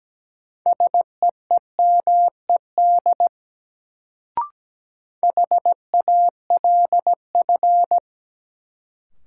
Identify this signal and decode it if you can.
Morse code, heard as a keyed tone, about 17 wpm, 700 Hz. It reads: SEEMED HALF
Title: Sets of 3 Words - New Word Drill - 17wpm